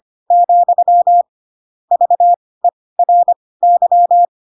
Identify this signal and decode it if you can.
Morse code, heard as a keyed tone, about 25 wpm, spaced as for 18 wpm, 700 Hz. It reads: , VERY
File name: Pinocchio / Pinocchio ch7 12 25 ik1hge